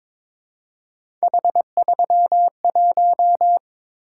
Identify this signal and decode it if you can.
Morse code, heard as a keyed tone, about 22 wpm, 700 Hz. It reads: H31
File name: practice / 22wpm / Three Letter-Number Combinations - 22wpm